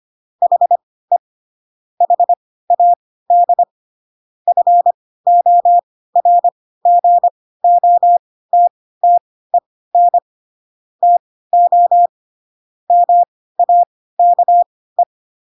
Morse code, heard as a keyed tone, about 25 wpm, spaced as for 16 wpm, 700 Hz. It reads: HE HAD FORGOTTEN TO MAKE
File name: Pinocchio / Pinocchio ch3 10 25 ik1hge